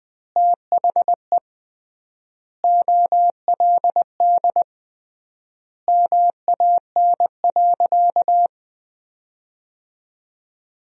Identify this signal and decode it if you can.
Morse code, heard as a keyed tone, about 20 wpm, 700 Hz. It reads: THE OLD MAN.
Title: 3x Word-Spacing - Sentences from Top 500 Words - 20wpm